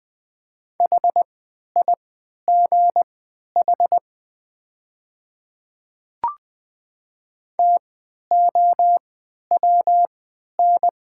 Morse code, heard as a keyed tone, about 20 wpm, spaced as for 11 wpm, 700 Hz. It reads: HIGH TOWN
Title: 3x Character Spacing - Top 500 Words - 20wpm